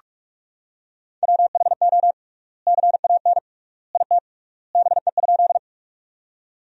Morse code, heard as a keyed tone, about 42 wpm, 700 Hz. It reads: WHO CAN IT BE?
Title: 3x Word-Spacing - Sentences from Top 100 Words - 45wpm